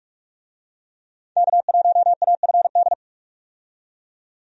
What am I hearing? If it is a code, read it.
Morse code, heard as a keyed tone, about 42 wpm, 700 Hz. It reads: K1AFD
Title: USA Call Signs - 5 Characters - 45wpm